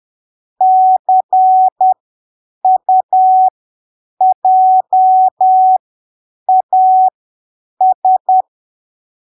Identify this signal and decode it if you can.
Morse code, heard as a keyed tone, about 10 wpm, spaced as for 7 wpm, 750 Hz. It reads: CUJAS
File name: arquivos / texto / texto 01 10ppm